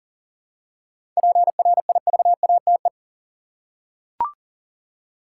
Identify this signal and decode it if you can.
Morse code, heard as a keyed tone, about 40 wpm, 700 Hz. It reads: PRIVATE
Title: Seven Letter Words - 40wpm